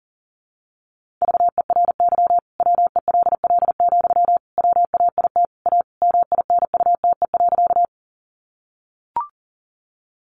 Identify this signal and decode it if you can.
Morse code, heard as a keyed tone, about 40 wpm, 700 Hz. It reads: VERY WELL, WAIT A MINUTE.
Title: Sentences from Top 500 Words 40wpm